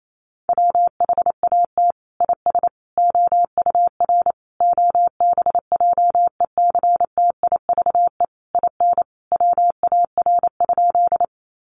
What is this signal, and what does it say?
Morse code, heard as a keyed tone, about 28 wpm, 700 Hz. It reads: WHAT IS OUR OBJECTIVE IN WAR?
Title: Sentences from Top 1200 Words 28wpm